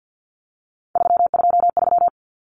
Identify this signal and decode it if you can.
Morse code, heard as a keyed tone, about 50 wpm, 700 Hz. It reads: 323
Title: Common Cut Numbers - 3-Digit 50wpm